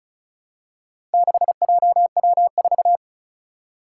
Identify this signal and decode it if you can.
Morse code, heard as a keyed tone, about 36 wpm, 700 Hz. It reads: 6JW4
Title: Four Challenging Letter-Number Combinations 35wpm